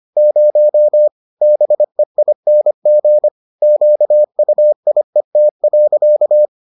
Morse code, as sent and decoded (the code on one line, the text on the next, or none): ----- / -... .. -. --. / --.- ..- .. - .-.-.-
0 BEING QUIET.